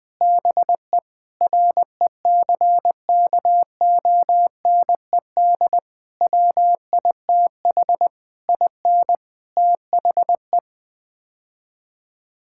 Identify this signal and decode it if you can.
Morse code, heard as a keyed tone, about 20 wpm, 700 Hz. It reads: BE RECKONED WITH IN THE
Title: Sets of 8 Words - 20wpm